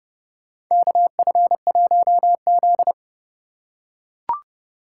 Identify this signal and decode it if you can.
Morse code, heard as a keyed tone, about 30 wpm, 700 Hz. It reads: KF1Z
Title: USA Call Signs - 4 Characters - 2x Character Spacing - 30wpm